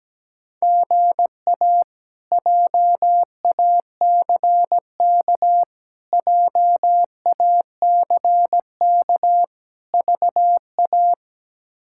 Morse code, heard as a keyed tone, about 17 wpm, 700 Hz. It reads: GA JACK JACK VA
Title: K1USN SST QSO Practice - 17wpm